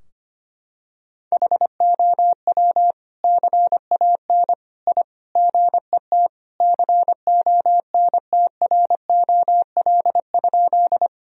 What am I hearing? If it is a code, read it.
Morse code, heard as a keyed tone, about 25 wpm, 700 Hz. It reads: HOW CAN I GET CONTROL?
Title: Sentences from Top 1000 Words - Review Part 3 - 25wpm